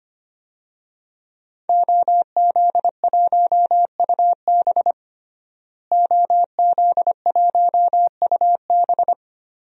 Morse code, heard as a keyed tone, about 25 wpm, 700 Hz. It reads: OZ1UB OZ1UB
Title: DXCC-100 Call Signs - 5 Characters - Repeated 2x - 25wpm